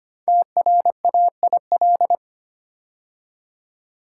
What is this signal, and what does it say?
Morse code, heard as a keyed tone, about 25 wpm, 700 Hz. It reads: TRAIL
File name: 5 Letter Words - First 3 Letters Given - 25wpm